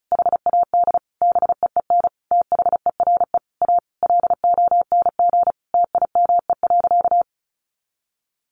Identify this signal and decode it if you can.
Morse code, heard as a keyed tone, about 34 wpm, 700 Hz. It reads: HAD BEEN THERE A LONG TIME.
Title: Sentences from Top 400 Words 35wpm